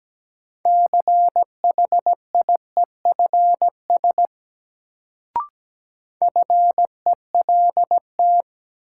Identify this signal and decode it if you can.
Morse code, heard as a keyed tone, about 17 wpm, 700 Hz. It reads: CHIEFS FELT
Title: Sets of 5 Words Encore - New Word Drill - 17wpm